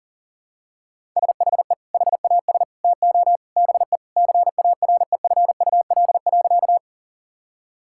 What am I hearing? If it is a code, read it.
Morse code, heard as a keyed tone, about 40 wpm, 700 Hz. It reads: SHE HAS TO BE CAREFUL.
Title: Sentences from Top 1600 Words 40wpm